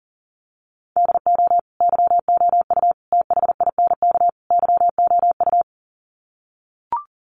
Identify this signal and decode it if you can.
Morse code, heard as a keyed tone, about 40 wpm, 700 Hz. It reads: DO YOU THINK YOU
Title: Sets of 4 Words 40wpm